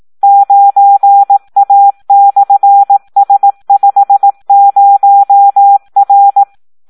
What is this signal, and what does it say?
Morse code, heard as a keyed tone, about 18 wpm, 800 Hz. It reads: 9A/S50R
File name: Audio practica CW / SD 1 18